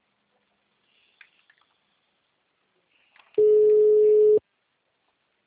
Ringback tone (Europe, 425 Hz).